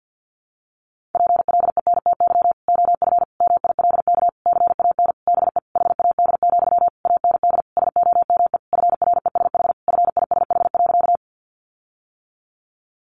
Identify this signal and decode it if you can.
Morse code, heard as a keyed tone, about 50 wpm, 700 Hz. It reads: PLENTY OF MILK CAN BE HAD, AND SOME FRESH FISH.
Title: Sentences from Top 1500 Words 50wpm